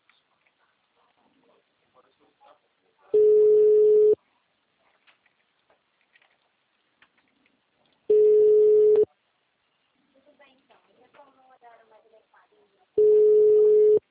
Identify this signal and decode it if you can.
Ringback tone (Europe, 425 Hz), with speech.
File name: out- -2002- - - .23968